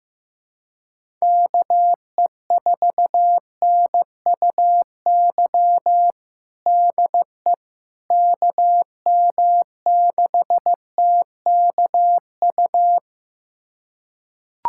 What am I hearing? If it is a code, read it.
Morse code, heard as a keyed tone, about 15 wpm, 700 Hz. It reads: KE4NUY DE KM6TKU
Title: USA Callsign DE Callsign 15wpm